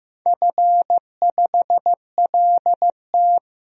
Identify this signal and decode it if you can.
Morse code, heard as a keyed tone, about 15 wpm, 700 Hz. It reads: F5LT
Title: DXCC-100 Call Signs - 4 Characters - 15wpm